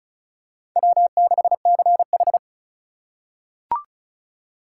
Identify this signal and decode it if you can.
Morse code, heard as a keyed tone, about 36 wpm, 700 Hz. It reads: W6CH
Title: USA Call Signs - 4 Characters - 4x Character Spacing - 35wpm